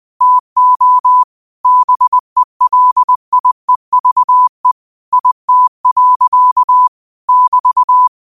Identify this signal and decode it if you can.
Morse code, heard as a keyed tone, about 20 wpm, 1000 Hz. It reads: TO BELIEVE IT. <BT>
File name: qotdfiles / 20wpm